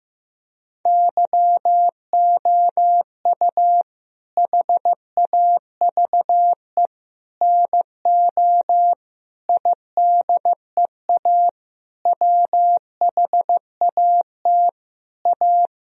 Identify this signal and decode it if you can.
Morse code, heard as a keyed tone, about 15 wpm, 700 Hz. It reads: YOU HAVE NO IDEA WHAT A
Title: Sentences from Top 1000 Words - Review Part 1 - 15wpm